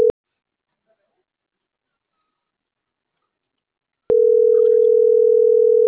Ringback tone (North America), 440 + 480 Hz.